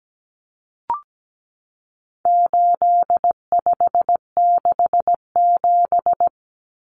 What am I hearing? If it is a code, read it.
Morse code, heard as a keyed tone, about 17 wpm, 700 Hz. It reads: 8567
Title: Numbers - 4-Digit 17wpm